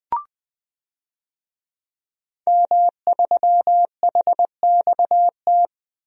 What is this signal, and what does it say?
Morse code, heard as a keyed tone, about 20 wpm, 700 Hz. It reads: M3HXT
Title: DXCC-100 Call Signs - 5 Characters - 20wpm